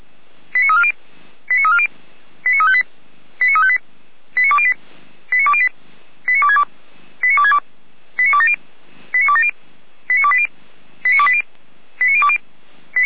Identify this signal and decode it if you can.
Five-tone selective call (ZVEI), heard as quick runs of five tones, tones of about 80 ms. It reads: 89340 89340 89358 89358 89208 89208 89262 89262 89270 89270 89280 89280 89020